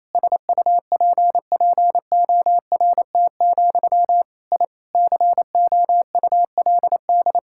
Morse code, heard as a keyed tone, about 28 wpm, 700 Hz. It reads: SUPPORT, I COULD
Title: Sentences from Top 1000 Words - Review Part 5 - 28wpm